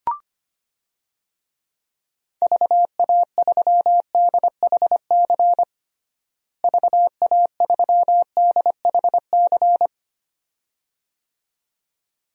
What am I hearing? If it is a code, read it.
Morse code, heard as a keyed tone, about 25 wpm, 700 Hz. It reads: VA3DHC VA3DHC
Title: DXCC-100 Call Signs - 6 Characters - Repeated 2x - 25wpm